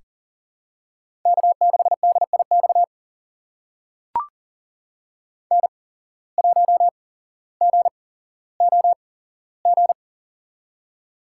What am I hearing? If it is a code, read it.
Morse code, heard as a keyed tone, about 39 wpm, spaced as for 17 wpm, 700 Hz. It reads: K6DIX N1GOG
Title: USA Call Signs - 5 Characters - 8x Character Spacing - 40wpm